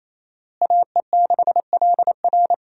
Morse code, heard as a keyed tone, about 28 wpm, 700 Hz. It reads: AE6LR
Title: USA Call Signs - 5 Characters - Repeated 2x - 28wpm